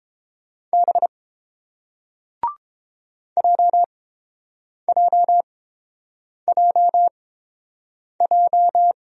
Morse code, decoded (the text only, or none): B JJJJ